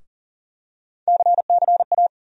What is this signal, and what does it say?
Morse code, heard as a keyed tone, about 40 wpm, 700 Hz. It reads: CCA
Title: USA Call Sign Postfixes - 3 Letters 40wpm